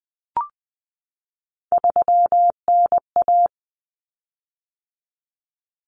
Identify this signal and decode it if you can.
Morse code, heard as a keyed tone, about 20 wpm, 700 Hz. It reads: 3NA
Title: Common Cut Numbers - 3-Digit 20wpm